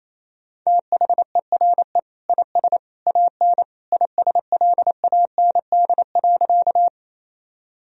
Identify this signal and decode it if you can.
Morse code, heard as a keyed tone, about 28 wpm, 700 Hz. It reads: THERE IS AN ISLAND.
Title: Sentences from Top 1000 Words - Review Part 3 - 28wpm